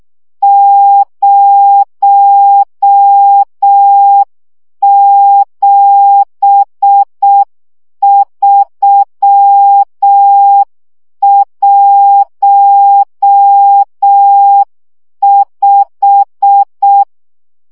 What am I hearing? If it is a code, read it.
Morse code, heard as a keyed tone, about 6 wpm, 800 Hz. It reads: 07315
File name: morze vizsga 2011-3 5